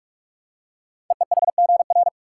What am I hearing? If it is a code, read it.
Morse code, heard as a keyed tone, about 44 wpm, 700 Hz. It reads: EEHGR
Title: Code Groups - 5 Letters - 45wpm